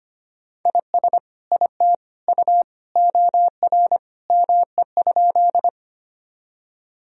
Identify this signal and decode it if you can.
Morse code, heard as a keyed tone, about 25 wpm, 700 Hz. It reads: IS IT U OR ME?
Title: Sentences from Top 200 Words and Abbreviations 25wpm